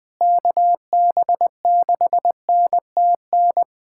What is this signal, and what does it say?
Morse code, heard as a keyed tone, about 20 wpm, 700 Hz. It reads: KB6NTN
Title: USA Callsign DE Callsign 20wpm